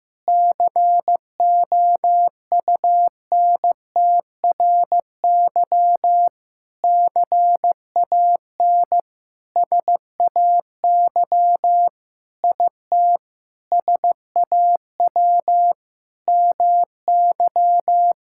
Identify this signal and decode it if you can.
Morse code, heard as a keyed tone, about 15 wpm, 700 Hz. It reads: COUNTRY CAN SAY IT SAW MY
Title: Sentences from Top 1000 Words - Review Part 5 - 15wpm